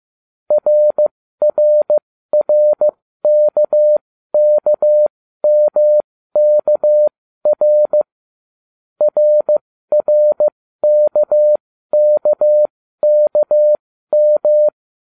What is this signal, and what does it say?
Morse code, heard as a keyed tone, about 15 wpm, spaced as for 13 wpm, 600 Hz. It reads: RRRKKMKR RRKKKM